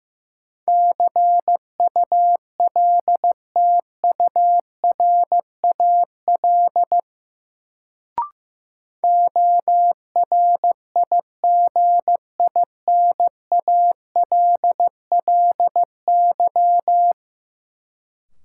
Morse code, heard as a keyed tone, about 15 wpm, 700 Hz. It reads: CULTURAL ORIGINALLY
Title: Sentences from Top 1500 Words - New Word Drill - 15wpm